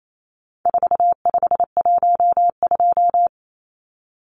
Morse code, heard as a keyed tone, about 28 wpm, 700 Hz. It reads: 4512
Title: Numbers - 4-Digit 28wpm